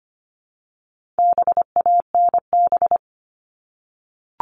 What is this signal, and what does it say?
Morse code, heard as a keyed tone, about 25 wpm, 700 Hz. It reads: BANB
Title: All Cut Numbers - 4-Digit 25wpm